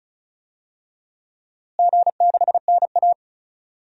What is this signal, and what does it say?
Morse code, heard as a keyed tone, about 36 wpm, 700 Hz. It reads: G6NA
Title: DXCC-100 Call Signs - 4 Characters - Repeated 2x - 35wpm